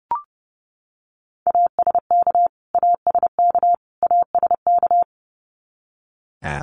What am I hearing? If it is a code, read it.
Morse code, heard as a keyed tone, about 30 wpm, 700 Hz. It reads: ASK ASK ASK
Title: Top 300 Words Repeated 30wpm